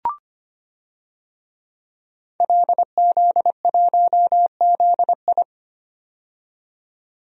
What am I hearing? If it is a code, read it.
Morse code, heard as a keyed tone, about 25 wpm, 700 Hz. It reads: LZ1ZI
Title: DXCC-100 Call Signs - 5 Characters - 25wpm